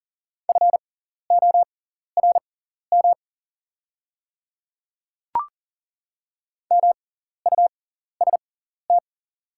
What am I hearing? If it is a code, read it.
Morse code, heard as a keyed tone, about 40 wpm, spaced as for 14 wpm, 700 Hz. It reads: FORM MUST
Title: 6x Character Spacing - Top 200 Words - 40wpm